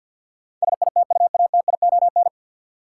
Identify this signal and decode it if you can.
Morse code, heard as a keyed tone, about 52 wpm, 700 Hz. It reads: SITUATION
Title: Sentences from Top 1200 Words - New Word Drill Alt - 50wpm